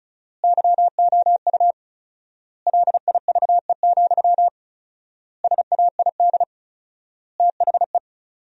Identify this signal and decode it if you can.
Morse code, heard as a keyed tone, about 34 wpm, 700 Hz. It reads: YOU LIVE, SAID THE
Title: 4x Word-Spacing - Sentences from Top 200 Words - 35wpm